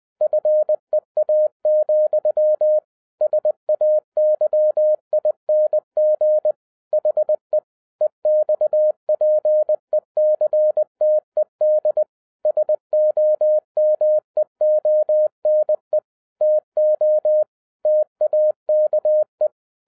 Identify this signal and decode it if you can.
Morse code, heard as a keyed tone, about 20 wpm, 610 Hz. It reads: FEA, SAYING HE EXPECTED SOMEONE TO TAKE
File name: 2012-01-mp3 / n16-THROUGH EIGHT MONTHS-20wpm